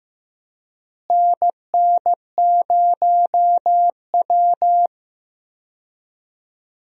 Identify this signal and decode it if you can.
Morse code, heard as a keyed tone, about 15 wpm, 700 Hz. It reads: NN0W
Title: USA Call Signs - 4 Characters - Repeated 3x - 15wpm